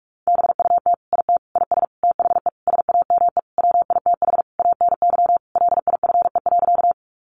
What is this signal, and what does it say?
Morse code, heard as a keyed tone, about 44 wpm, 700 Hz. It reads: BUT IT IS THE SAME WITH ANY LIFE.
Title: Sentences from Top 300 Words 45wpm